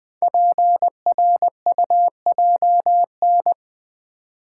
Morse code, heard as a keyed tone, about 20 wpm, 700 Hz. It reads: PRUJN
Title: Code Groups - 5 Letters - 20wpm